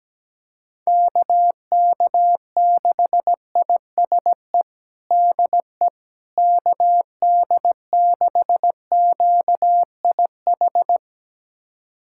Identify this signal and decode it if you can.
Morse code, heard as a keyed tone, about 17 wpm, 700 Hz. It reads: KK6ISE DE KD6QIH
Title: USA Callsign DE Callsign - 6 Characters - 17wpm